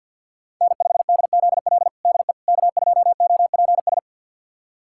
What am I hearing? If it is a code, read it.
Morse code, heard as a keyed tone, about 53 wpm, 700 Hz. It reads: N5DZL DE K2OPS
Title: USA Callsign DE Callsign - 5 Characters - 50wpm